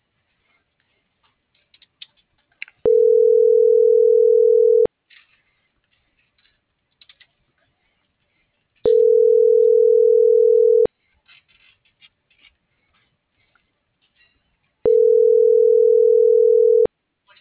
Ringback tone (North America), 440 + 480 Hz.